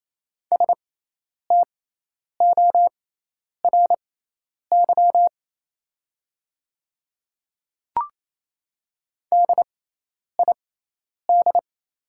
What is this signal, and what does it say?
Morse code, heard as a keyed tone, about 28 wpm, spaced as for 10 wpm, 690 Hz. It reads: STORY DID
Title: 6x Character Spacing - Top 300 Words - 28wpm